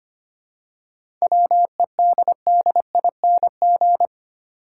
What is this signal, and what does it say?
Morse code, heard as a keyed tone, about 25 wpm, 700 Hz. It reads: WEDDING
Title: Seven Letter Words - 25wpm